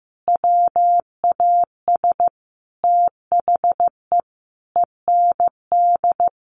Morse code, heard as a keyed tone, about 15 wpm, 700 Hz. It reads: WAS THE END
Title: Sentences from Top 200 Words 15wpm